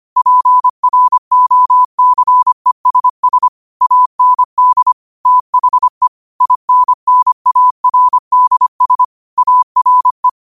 Morse code, heard as a keyed tone, about 25 wpm, 1000 Hz. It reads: PROCESS AND THE INNARDS ARE
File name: qotdfiles / 25wpm